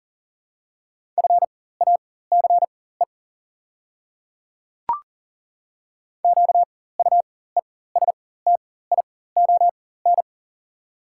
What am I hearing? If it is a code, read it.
Morse code, heard as a keyed tone, about 40 wpm, spaced as for 19 wpm, 690 Hz. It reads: FACE QUESTION
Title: 4x Character Spacing - Top 500 Words - 40wpm